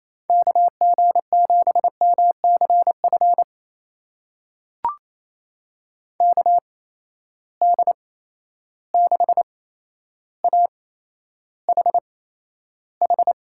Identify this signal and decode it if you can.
Morse code, heard as a keyed tone, about 28 wpm, spaced as for 12 wpm, 700 Hz. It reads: KG7MCF KD6AHH
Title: USA Call Signs - 6 Characters - 8x Character Spacing - 28wpm